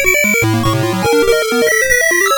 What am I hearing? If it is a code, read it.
There is no code.